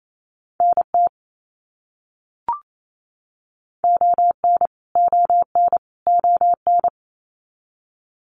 Morse code, heard as a keyed tone, about 28 wpm, 700 Hz. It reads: NT ON ON ON
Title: Canadian Provinces and Territories - Repeated 3x - 28wpm